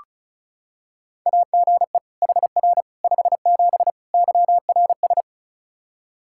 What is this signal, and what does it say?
Morse code, heard as a keyed tone, about 34 wpm, 700 Hz. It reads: AGE HR 57 YRS
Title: Practice QSO - Exchange 3 - More Info - 35wpm